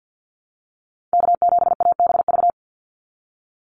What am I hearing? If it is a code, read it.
Morse code, heard as a keyed tone, about 50 wpm, 700 Hz. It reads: K7ABV
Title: Call Signs Repeated 3x - 50wpm